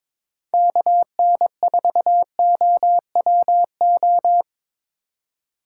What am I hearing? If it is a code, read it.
Morse code, heard as a keyed tone, about 26 wpm, 700 Hz. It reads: KN4OWO
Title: Speed Racing - USA 2x3 Call Signs 22wpm